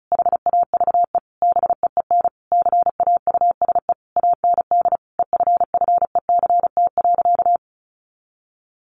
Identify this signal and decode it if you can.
Morse code, heard as a keyed tone, about 34 wpm, 700 Hz. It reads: HAVE BEEN CAUSE AND EFFECT.
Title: Sentences from Top 1000 Words 35wpm